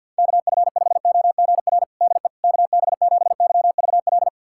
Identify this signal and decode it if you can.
Morse code, heard as a keyed tone, about 52 wpm, 700 Hz. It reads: KF5OGR DE KB7YVL
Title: USA Callsign DE Callsign - 6 Characters - 50wpm